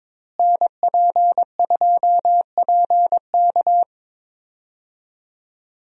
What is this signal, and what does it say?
Morse code, heard as a keyed tone, about 21 wpm, 700 Hz. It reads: NP2PK
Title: USA Call Signs - 5 Characters - Repeated 2x - 22wpm